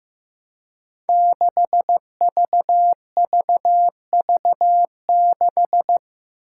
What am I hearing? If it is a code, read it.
Morse code, heard as a keyed tone, about 15 wpm, 700 Hz. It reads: 6VVV6
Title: Five Challenging Letter-Number Combinations 15wpm